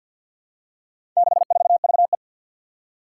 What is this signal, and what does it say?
Morse code, heard as a keyed tone, about 50 wpm, 700 Hz. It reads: 64VE
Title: Four Challenging Letter-Number Combinations 50wpm